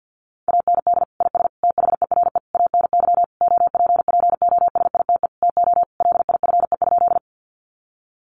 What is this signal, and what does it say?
Morse code, heard as a keyed tone, about 50 wpm, 700 Hz. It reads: AND IS THERE ANY OPPOSITE TO LIFE?
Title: Sentences from Top 1300 Words 50wpm